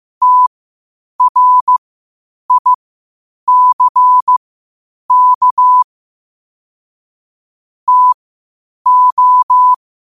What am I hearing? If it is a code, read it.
Morse code, heard as a keyed tone, about 15 wpm, spaced as for 8 wpm, 1000 Hz. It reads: TRICK TO
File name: qotdfiles / 8wpm